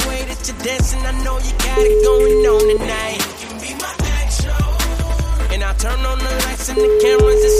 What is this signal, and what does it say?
Ringback tone (Europe, 425 Hz), with noise.